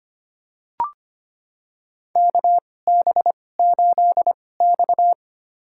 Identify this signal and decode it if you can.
Morse code, heard as a keyed tone, about 25 wpm, spaced as for 18 wpm, 700 Hz. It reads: KB8X
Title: USA Call Signs - 4 Characters - 2x Character Spacing - 25wpm